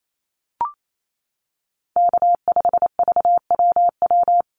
Morse code, heard as a keyed tone, about 28 wpm, 700 Hz. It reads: K5VWW
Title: Call Signs - 28wpm